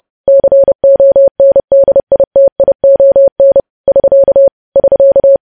Morse code, heard as a keyed tone, about 30 wpm, 550 Hz. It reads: CONDITION <SK> <SK>
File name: podcast / News headlines-1x-30x30